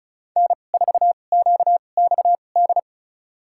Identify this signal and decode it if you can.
Morse code, heard as a keyed tone, about 34 wpm, spaced as for 25 wpm, 700 Hz. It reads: N4QXD
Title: USA Call Signs - 5 Characters - 2x Character Spacing - 35wpm